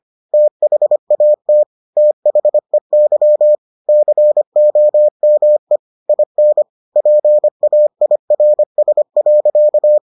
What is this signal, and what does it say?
Morse code, heard as a keyed tone, about 25 wpm, 600 Hz. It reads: THAT THEY COME IN PAIRS.